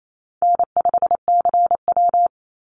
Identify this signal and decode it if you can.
Morse code, heard as a keyed tone, about 28 wpm, 700 Hz. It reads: N5CW
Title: Easy Call Signs Repeated 3x - 28wpm